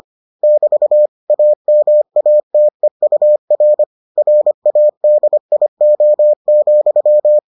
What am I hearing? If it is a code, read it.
Morse code, heard as a keyed tone, about 25 wpm, 600 Hz. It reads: <BT> AMATEUR RADIO,